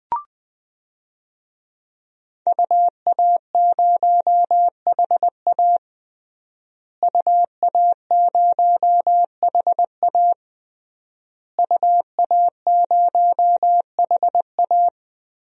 Morse code, heard as a keyed tone, about 20 wpm, 700 Hz. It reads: UA0HA UA0HA UA0HA
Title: DXCC-100 Call Signs - 5 Characters - Repeated 3x - 20wpm